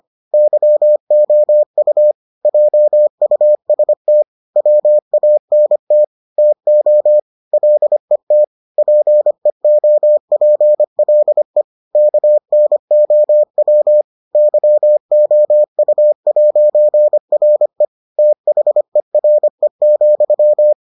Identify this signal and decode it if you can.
Morse code, heard as a keyed tone, about 25 wpm, 600 Hz. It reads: YOU JUST WANT TO LET PEOPLE KNOW YOU'RE THERE,